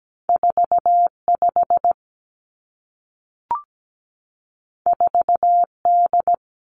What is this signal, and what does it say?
Morse code, heard as a keyed tone, about 17 wpm, 700 Hz. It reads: 45 4D
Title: Two Challenging Letter-Number Combinations 17wpm